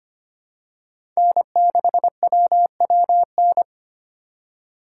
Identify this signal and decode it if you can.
Morse code, heard as a keyed tone, about 25 wpm, 700 Hz. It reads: N6WWN
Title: USA Call Signs - 3x Character Spacing - 25wpm